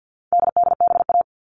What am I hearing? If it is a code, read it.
Morse code, heard as a keyed tone, about 50 wpm, 700 Hz. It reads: DDBA
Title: All Cut Numbers - 4-Digit 50wpm